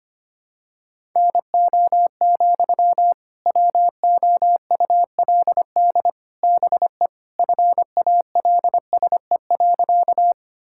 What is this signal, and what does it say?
Morse code, heard as a keyed tone, about 25 wpm, 700 Hz. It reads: NO, WOULD BE FALSE.
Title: Sentences from Top 1600 Words 25wpm